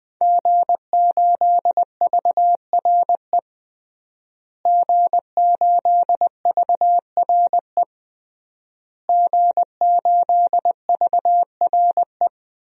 Morse code, heard as a keyed tone, about 20 wpm, 700 Hz. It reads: G8VRE G8VRE G8VRE
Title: DXCC-100 Call Signs - 5 Characters - Repeated 3x - 20wpm